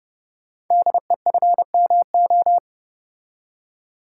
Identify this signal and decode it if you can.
Morse code, heard as a keyed tone, about 30 wpm, 700 Hz. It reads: DEFMO
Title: Code Groups - 5 Letters - 30wpm